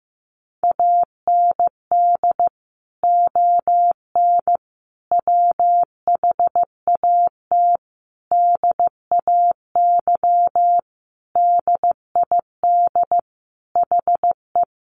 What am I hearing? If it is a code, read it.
Morse code, heard as a keyed tone, about 15 wpm, 700 Hz. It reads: AND ON WHAT DAY DID HE
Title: Sentences from Top 1100 Words 15wpm